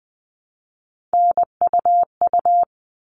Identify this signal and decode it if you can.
Morse code, heard as a keyed tone, about 20 wpm, 700 Hz. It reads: NUU
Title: All Cut Numbers - 3-Digit 20wpm